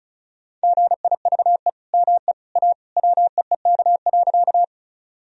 Morse code, heard as a keyed tone, about 36 wpm, 700 Hz. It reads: GIVE ME A WEEK.
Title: Sentences from Top 1000 Words - Review Part 2 - 35wpm